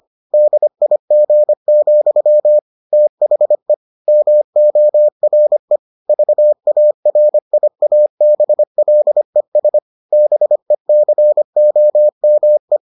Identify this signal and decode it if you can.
Morse code, heard as a keyed tone, about 25 wpm, 600 Hz. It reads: DIG, THE MORE VARIABLES BECOME